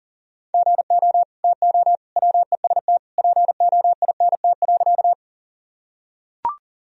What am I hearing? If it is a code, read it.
Morse code, heard as a keyed tone, about 40 wpm, 700 Hz. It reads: GO TO WEST POINT.
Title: Sentences from Top 1000 Words - Review Part 2 - 40wpm